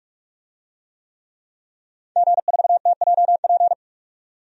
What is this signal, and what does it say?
Morse code, heard as a keyed tone, about 42 wpm, 700 Hz. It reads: G4TJP
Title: DXCC-100 Call Signs - 5 Characters - 45wpm